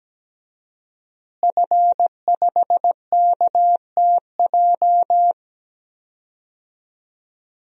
Morse code, heard as a keyed tone, about 17 wpm, 700 Hz. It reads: F5KTJ
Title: DXCC-100 Call Signs - 5 Characters - Repeated 3x - 17wpm